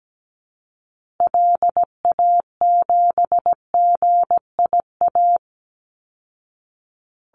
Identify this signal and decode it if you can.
Morse code, heard as a keyed tone, about 17 wpm, 700 Hz. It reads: LA7GIA
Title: Call Signs Repeated 2x - 17wpm